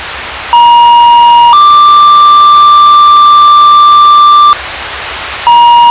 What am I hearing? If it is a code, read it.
Two-tone page, heard as a one-second tone, then a three-second tone, about 950 then 1180 Hz, with noise. It is heard from 0.5 to 4.5 s.